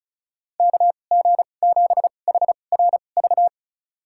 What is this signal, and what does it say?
Morse code, heard as a keyed tone, about 34 wpm, spaced as for 25 wpm, 700 Hz. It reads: KG7HRV
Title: USA Call Signs - 2x Character Spacing - 35wpm